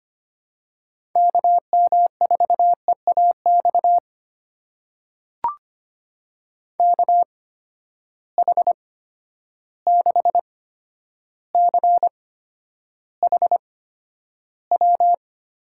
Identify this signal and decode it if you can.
Morse code, heard as a keyed tone, about 25 wpm, spaced as for 11 wpm, 700 Hz. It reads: KM4EAX KH6CHW
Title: USA Call Signs - 6 Characters - 8x Character Spacing - 25wpm